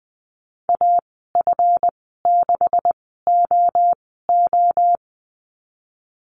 Morse code, heard as a keyed tone, about 20 wpm, spaced as for 14 wpm, 700 Hz. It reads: AF6OO
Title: 147 Lesson 24g - Letter y - Call Signs 20wpm